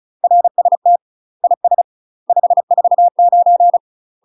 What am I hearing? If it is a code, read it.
Morse code, heard as a keyed tone, about 36 wpm, 700 Hz. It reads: RST IS 549
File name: UK MASTER QSOs TRAINING / 35WPM / UK MASTER QSOs 0009